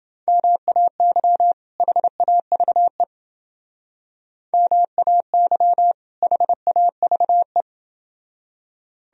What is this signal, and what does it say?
Morse code, heard as a keyed tone, about 29 wpm, 700 Hz. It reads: MAY HAVE MAY HAVE